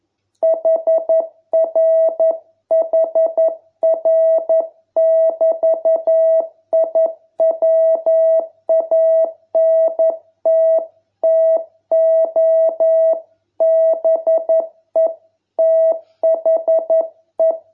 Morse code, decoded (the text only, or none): HRHR<BT>I WANT TO BE THE